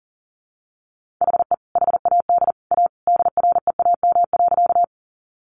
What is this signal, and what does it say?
Morse code, heard as a keyed tone, about 40 wpm, 700 Hz. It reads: HE HAD A DREAM.
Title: Sentences from Top 1500 Words 40wpm